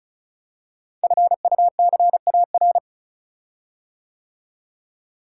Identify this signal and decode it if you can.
Morse code, heard as a keyed tone, about 36 wpm, 690 Hz. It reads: FUCAR
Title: Code Groups - 5 Letter:Numbers - 35wpm